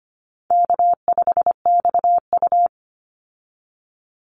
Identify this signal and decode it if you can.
Morse code, heard as a keyed tone, about 25 wpm, 700 Hz. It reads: K5XU
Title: Easy Call Signs - 25wpm